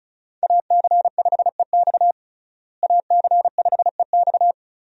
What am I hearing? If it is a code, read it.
Morse code, heard as a keyed tone, about 36 wpm, 700 Hz. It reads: AC5EX AC5EX
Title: USA Call Signs - 5 Characters - Repeated 2x - 35wpm